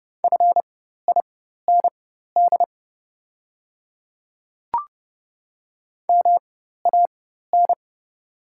Morse code, heard as a keyed tone, about 31 wpm, spaced as for 14 wpm, 690 Hz. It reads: FIND MAN
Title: 4x Character Spacing - Top 200 Words - 30wpm